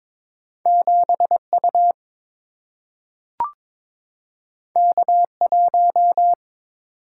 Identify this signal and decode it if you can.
Morse code, heard as a keyed tone, about 22 wpm, 700 Hz. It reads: 7 U K 1
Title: Single-Letter Single-Number 22wpm